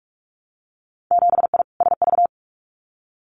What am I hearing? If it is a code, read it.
Morse code, heard as a keyed tone, about 46 wpm, 700 Hz. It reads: 7I SV
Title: Winter Field Day Exchanges Encore 45wpm